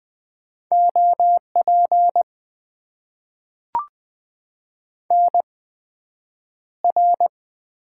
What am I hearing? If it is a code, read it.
Morse code, heard as a keyed tone, about 20 wpm, spaced as for 9 wpm, 700 Hz. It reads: OP NR
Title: Top 90 QSO Elements - 8x Character Spacing - 20wpm